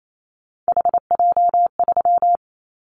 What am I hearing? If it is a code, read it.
Morse code, heard as a keyed tone, about 28 wpm, 700 Hz. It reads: HJ3
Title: Callsign Prefixes for Top 100 DXCC - 28wpm